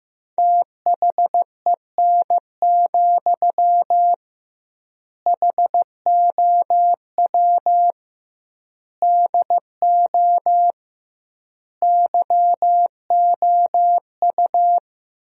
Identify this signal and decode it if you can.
Morse code, heard as a keyed tone, about 16 wpm, 700 Hz. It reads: THEN, HOW DO YOU
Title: 2x Word-Spacing - Sentences from Top 100 Words - 15wpm